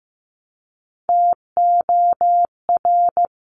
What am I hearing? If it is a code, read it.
Morse code, heard as a keyed tone, about 15 wpm, 690 Hz. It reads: TOR
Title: Top Three Letter Combinations in Words 15wpm